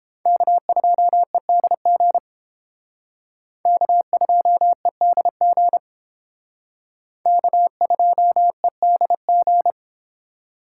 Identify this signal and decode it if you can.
Morse code, heard as a keyed tone, about 30 wpm, 700 Hz. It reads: K2EDG K2EDG K2EDG